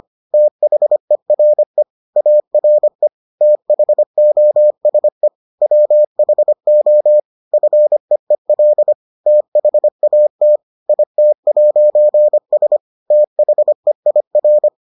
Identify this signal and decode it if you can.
Morse code, heard as a keyed tone, about 25 wpm, 600 Hz. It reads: THERE ARE THOSE WHO FEEL THAT IT'S THEIR